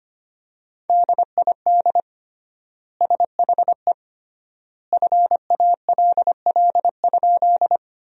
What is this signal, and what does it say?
Morse code, heard as a keyed tone, about 25 wpm, 700 Hz. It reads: DID SHE FALL?
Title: 3x Word-Spacing - Sentences from Top 500 Words - 25wpm